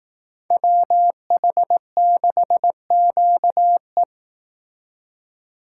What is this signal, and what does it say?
Morse code, heard as a keyed tone, about 18 wpm, 700 Hz. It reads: WH6QE
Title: Speed Racing - USA 2x2 Call Signs 15wpm